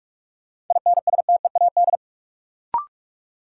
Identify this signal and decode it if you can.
Morse code, heard as a keyed tone, about 42 wpm, 700 Hz. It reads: INSTEAD
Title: Sentences from Top 800 Words - New Word Drill - 45wpm